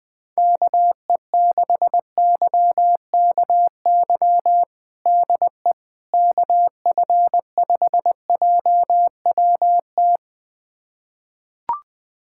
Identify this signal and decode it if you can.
Morse code, heard as a keyed tone, about 20 wpm, 700 Hz. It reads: KE6YKY DE KF5JWT
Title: USA Callsign DE Callsign - 6 Characters - 20wpm